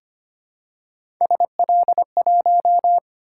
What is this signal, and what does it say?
Morse code, heard as a keyed tone, about 25 wpm, 700 Hz. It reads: SL1
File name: Three Letter-Number Combinations - 25wpm